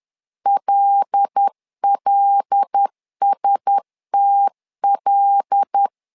Morse code, heard as a keyed tone, about 10 wpm, 800 Hz. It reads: LLSTL
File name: LEZIONI / 5 LBVC / 5 L 10wpm 2259